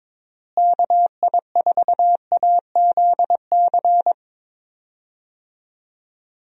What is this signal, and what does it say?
Morse code, heard as a keyed tone, about 22 wpm, 700 Hz. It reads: KI4AZC
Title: USA Call Signs - 6 Characters - Repeated 3x - 22wpm